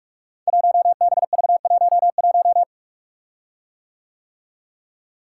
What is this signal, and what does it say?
Morse code, heard as a keyed tone, about 44 wpm, 700 Hz. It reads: 1BV11
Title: Code Groups - 5 Letter:Numbers - 45wpm